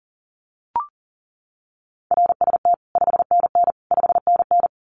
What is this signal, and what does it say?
Morse code, heard as a keyed tone, about 40 wpm, 700 Hz. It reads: RST 5NN 5NN
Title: Common QSO Phrases Encore 40wpm